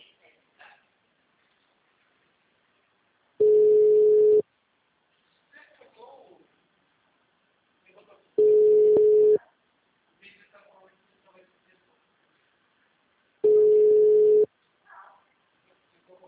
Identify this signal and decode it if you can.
Ringback tone (Europe, 425 Hz), with speech.